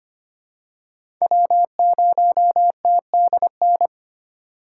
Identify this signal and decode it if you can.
Morse code, heard as a keyed tone, about 25 wpm, 700 Hz. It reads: W0TDN
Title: USA Call Signs - 5 Characters - 4x Character Spacing - 25wpm